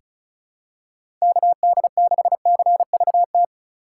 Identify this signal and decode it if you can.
Morse code, heard as a keyed tone, about 36 wpm, 700 Hz. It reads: KD6CVT
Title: USA Call Signs - 6 Characters - 3x Character Spacing - 35wpm